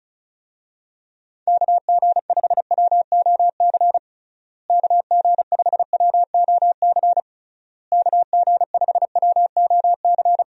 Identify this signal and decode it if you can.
Morse code, heard as a keyed tone, about 36 wpm, 700 Hz. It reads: KG5WOC KG5WOC KG5WOC
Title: USA Call Signs - 6 Characters - Repeated 3x - 35wpm